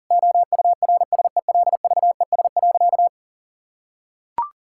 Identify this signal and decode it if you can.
Morse code, heard as a keyed tone, about 40 wpm, 700 Hz. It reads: OURSELVES.
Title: Sentences from Top 1600 Words 40wpm